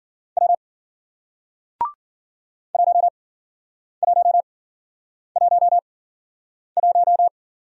Morse code, heard as a keyed tone, about 53 wpm, spaced as for 9 wpm, 700 Hz. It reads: W 1111